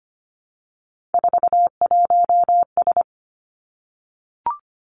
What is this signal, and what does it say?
Morse code, heard as a keyed tone, about 25 wpm, 700 Hz. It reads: 41S
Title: Three Challenging Letter-Number Combinations 25wpm